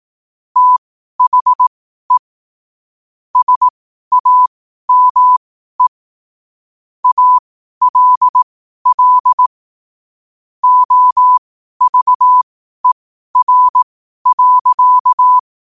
Morse code, heard as a keyed tone, about 18 wpm, spaced as for 12 wpm, 1000 Hz. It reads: THE SAME ALL OVER.
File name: qotdfiles / 12wpm